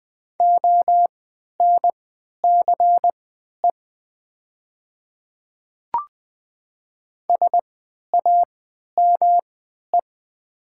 Morse code, heard as a keyed tone, about 20 wpm, spaced as for 11 wpm, 700 Hz. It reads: ONCE SAME
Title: 3x Character Spacing - Top 500 Words - 20wpm